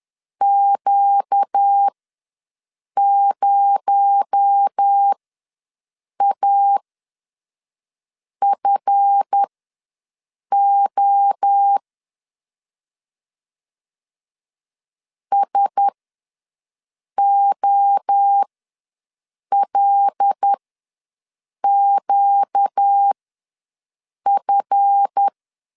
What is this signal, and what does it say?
Morse code, heard as a keyed tone, about 10 wpm, spaced as for 5 wpm, 800 Hz. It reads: Q0AFO SOLQF
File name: LEZIONI / 7 GF9Z / 7 GF ws 10wpm 3918